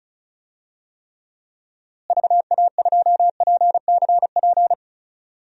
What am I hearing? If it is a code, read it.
Morse code, heard as a keyed tone, about 36 wpm, 700 Hz. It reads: VA2PCP
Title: DXCC-100 Call Signs - 6 Characters - Repeated 3x - 35wpm